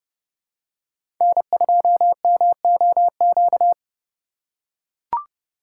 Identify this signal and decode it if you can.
Morse code, heard as a keyed tone, about 32 wpm, 700 Hz. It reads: N2MOQ
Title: USA Call Signs - 4x Character Spacing - 30wpm